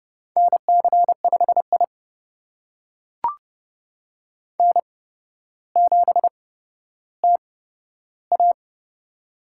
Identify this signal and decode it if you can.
Morse code, heard as a keyed tone, about 30 wpm, spaced as for 13 wpm, 700 Hz. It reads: NC5I N7TA
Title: USA Call Signs - 4 Characters - 8x Character Spacing - 30wpm